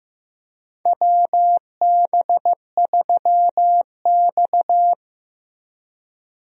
Morse code, heard as a keyed tone, about 15 wpm, 700 Hz. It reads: WB3X
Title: USA Call Signs - 4 Characters - Repeated 3x - 15wpm